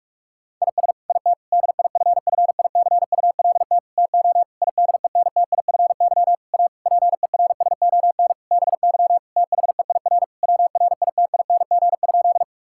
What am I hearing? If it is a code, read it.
Morse code, heard as a keyed tone, about 46 wpm, 700 Hz. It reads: IS IT DIFFICULT TO IDENTIFY A PERSON BY THEIR WRITING?
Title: Sentences from Top 1000 Words - Review Part 6 - 45wpm